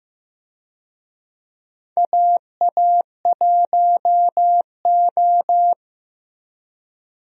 Morse code, heard as a keyed tone, about 15 wpm, 700 Hz. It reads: AA1O
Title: USA Call Signs - 4 Characters - Repeated 2x - 15wpm